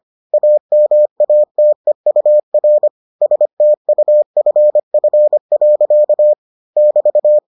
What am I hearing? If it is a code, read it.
Morse code, heard as a keyed tone, about 25 wpm, 600 Hz. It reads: AMATEUR STUFF. <BT>